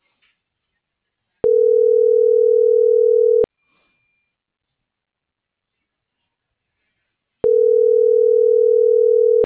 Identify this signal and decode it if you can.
Ringback tone (North America), 440 + 480 Hz.